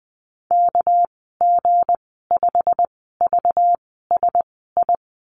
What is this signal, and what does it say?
Morse code, heard as a keyed tone, about 20 wpm, spaced as for 14 wpm, 700 Hz. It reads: KG5VSI
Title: 219 Lesson 35g - Number 0 - Call Signs 20wpm